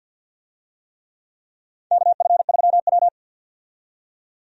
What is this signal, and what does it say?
Morse code, heard as a keyed tone, about 53 wpm, 700 Hz. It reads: KF3W
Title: USA Call Signs - 4 Characters - 50wpm